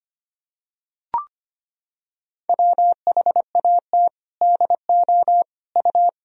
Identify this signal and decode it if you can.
Morse code, heard as a keyed tone, about 25 wpm, 700 Hz. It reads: WHAT DO U
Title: Sentences from Top 200 Words and Abbreviations 25wpm